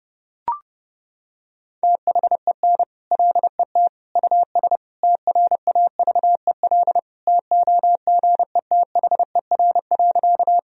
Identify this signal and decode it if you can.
Morse code, heard as a keyed tone, about 30 wpm, 700 Hz. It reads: THEN LET US TRAVEL TOGETHER.
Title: Sentences from Top 1000 Words - Review Part 2 - 30wpm